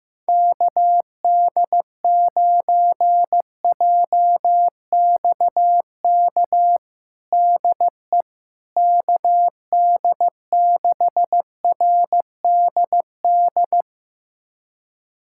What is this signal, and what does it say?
Morse code, heard as a keyed tone, about 15 wpm, 700 Hz. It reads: KD9JXK DE KD6RDD
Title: USA Callsign DE Callsign - 6 Characters - 15wpm